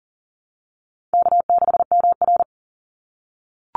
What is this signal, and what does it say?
Morse code, heard as a keyed tone, about 40 wpm, 700 Hz. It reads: K6MR
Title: Call Signs - 40wpm